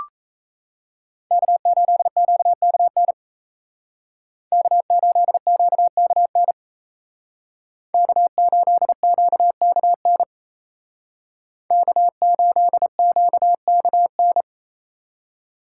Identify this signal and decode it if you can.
Morse code, heard as a keyed tone, about 34 wpm, 700 Hz. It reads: K8QKN K8QKN K8QKN K8QKN
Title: Speed Racing - USA Call Signs 28wpm